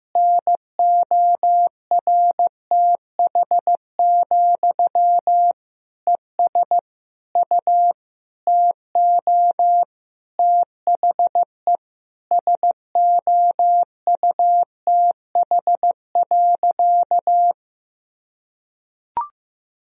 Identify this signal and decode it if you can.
Morse code, heard as a keyed tone, about 15 wpm, 700 Hz. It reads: NORTH, ES U TO THE SOUTH.
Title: Sentences from Top 400 Words and Abbreviations 15wpm